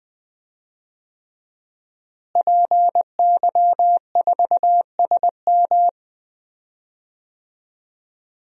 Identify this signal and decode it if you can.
Morse code, heard as a keyed tone, about 20 wpm, 700 Hz. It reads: PY4SM
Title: DXCC-100 Call Signs - 5 Characters - Repeated 3x - 20wpm